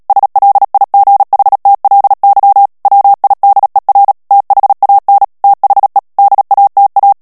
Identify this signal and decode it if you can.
Morse code, heard as a keyed tone, about 37 wpm, 800 Hz. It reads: SLIGHTLY WIDER THAN THE DATA
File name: morse mp3 / ANGLAIS texte clair 3 vitesse 35 mots minute